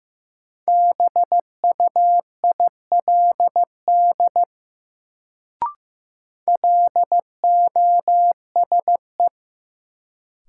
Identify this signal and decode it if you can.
Morse code, heard as a keyed tone, about 16 wpm, 700 Hz. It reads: BUILD LOSE
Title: Sentences from Top 1100 Words - New Word Drill Alt - 15wpm